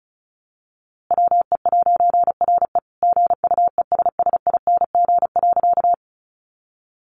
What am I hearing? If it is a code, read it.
Morse code, heard as a keyed tone, about 34 wpm, 700 Hz. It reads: WE'RE GUESSING.
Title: A Sound Of Thunder 35wpm